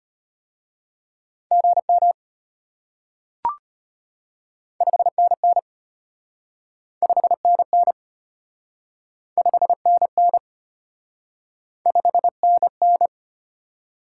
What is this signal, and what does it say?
Morse code, heard as a keyed tone, about 33 wpm, 700 Hz. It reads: GM 5NN 5NN 5NN 5NN